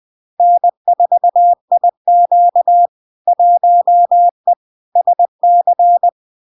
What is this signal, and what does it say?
Morse code, heard as a keyed tone, about 20 wpm, 700 Hz. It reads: N4IQ 1E SC
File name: LARK-FD-2022-Log-in-CW-20wpm-700-Hz